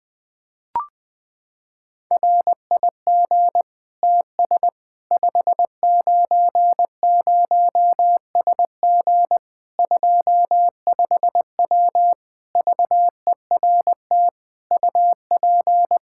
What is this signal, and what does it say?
Morse code, heard as a keyed tone, about 20 wpm, 700 Hz. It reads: RIG TS 590SG 25W VERT UP
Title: Practice QSO - Exchange 2 - Info - 20wpm